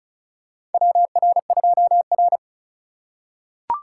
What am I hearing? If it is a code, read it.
Morse code, heard as a keyed tone, about 34 wpm, 700 Hz. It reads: WR2R
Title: USA Call Signs - 4 Characters - 2x Character Spacing - 35wpm